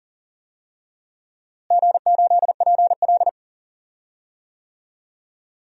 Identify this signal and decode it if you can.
Morse code, heard as a keyed tone, about 40 wpm, 700 Hz. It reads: G8PL
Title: DXCC-100 Call Signs - 4 Characters - Repeated 3x - 40wpm